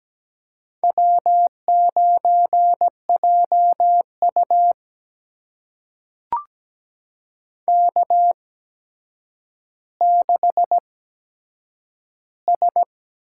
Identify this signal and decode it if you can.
Morse code, heard as a keyed tone, about 17 wpm, spaced as for 8 wpm, 700 Hz. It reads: W9JU K6S
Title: USA Call Signs - 4 Characters - 8x Character Spacing - 17wpm